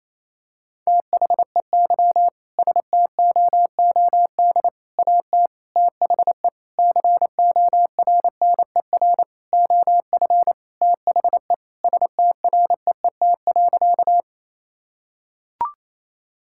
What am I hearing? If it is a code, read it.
Morse code, heard as a keyed tone, about 28 wpm, 700 Hz. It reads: THEY STOOD AT THE CORNER OF THE STREET.
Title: Sentences from Top 1600 Words 28wpm